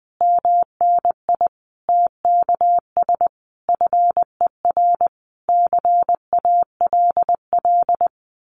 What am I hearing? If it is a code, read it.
Morse code, heard as a keyed tone, about 20 wpm, 700 Hz. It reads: MNI TKS FER CALL
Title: Common QSO Phrases 20wpm